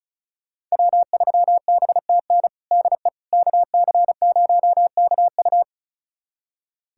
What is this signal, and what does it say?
Morse code, heard as a keyed tone, about 36 wpm, 700 Hz. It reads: W3BTN DE KC0KU
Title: USA Callsign DE Callsign - 5 Characters - 35wpm